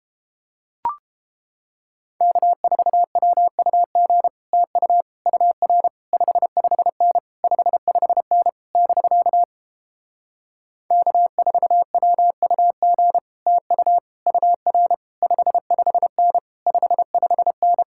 Morse code, heard as a keyed tone, about 31 wpm, 700 Hz. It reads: K4WUG TU UR 55N 55N <BK> K4WUG TU UR 55N 55N